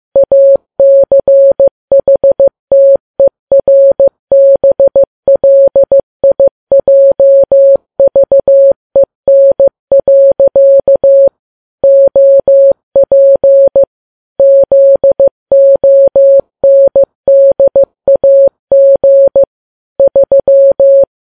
Morse code, heard as a keyed tone, about 15 wpm, 550 Hz. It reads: ACHTERBLIJVEN. OP ZONDAG 3
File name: Files Web tls / Morse on9fv / morse15kps